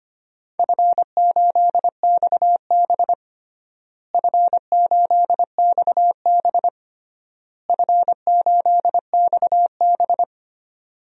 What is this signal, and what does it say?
Morse code, heard as a keyed tone, about 25 wpm, 700 Hz. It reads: F8XB F8XB F8XB
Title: DXCC-100 Call Signs - 4 Characters - Repeated 3x - 25wpm